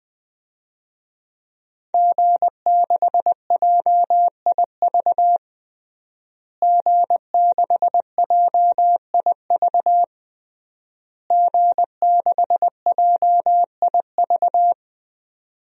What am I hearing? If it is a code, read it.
Morse code, heard as a keyed tone, about 20 wpm, 700 Hz. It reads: G6JIV G6JIV G6JIV
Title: DXCC-100 Call Signs - 5 Characters - Repeated 3x - 20wpm